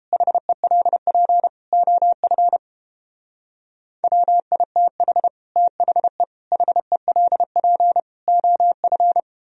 Morse code, heard as a keyed tone, about 29 wpm, 700 Hz. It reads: HELP OF WITH THE HELP OF